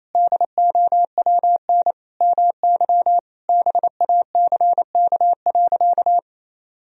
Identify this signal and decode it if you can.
Morse code, heard as a keyed tone, about 28 wpm, 700 Hz. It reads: DOWN MY BACK.
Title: Sentences from Top 1000 Words - Review Part 3 - 28wpm